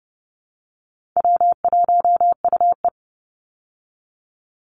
Morse code, heard as a keyed tone, about 30 wpm, 700 Hz. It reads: W1UE
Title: Call Signs Repeated 3x - 30wpm